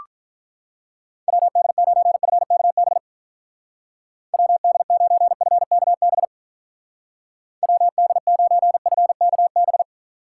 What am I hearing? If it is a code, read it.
Morse code, heard as a keyed tone, about 44 wpm, 700 Hz. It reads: WD9FKB WD9FKB WD9FKB